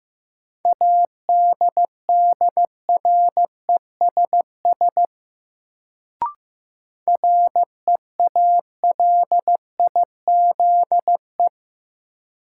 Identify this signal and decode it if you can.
Morse code, heard as a keyed tone, about 15 wpm, 700 Hz. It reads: ADDRESS REALIZE
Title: Sentences from Top 1000 Words - New Word Drill - 15wpm